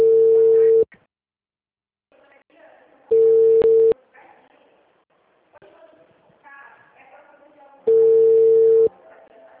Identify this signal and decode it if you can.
Ringback tone (China, 450 Hz), with speech.
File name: out- -2024